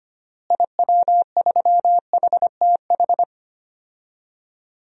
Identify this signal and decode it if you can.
Morse code, heard as a keyed tone, about 25 wpm, 700 Hz. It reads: IW3HTH
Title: DXCC-100 Call Signs - 6 Characters - Repeated 3x - 25wpm